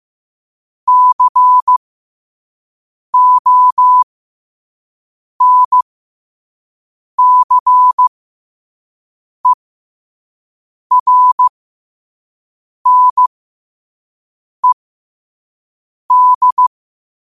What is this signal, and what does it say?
Morse code, heard as a keyed tone, about 15 wpm, spaced as for 5 wpm, 1000 Hz. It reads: CONCERNED
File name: qotdfiles / 5wpm